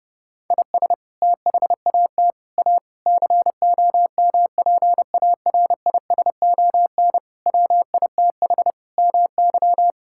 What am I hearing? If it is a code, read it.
Morse code, heard as a keyed tone, about 30 wpm, 700 Hz. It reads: IS THAT A COMPARISON WITH MY
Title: Sentences from Top 1600 Words 30wpm